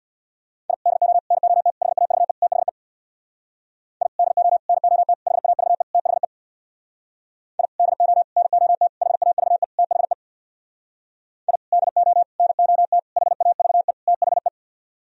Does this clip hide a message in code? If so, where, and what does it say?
Yes: Morse code from 0.7 to 14.5 s, heard as a keyed tone, about 64 wpm, 700 Hz. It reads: I DO NOT HAVE THE I DO NOT HAVE THE I DO NOT HAVE THE I DO NOT HAVE THE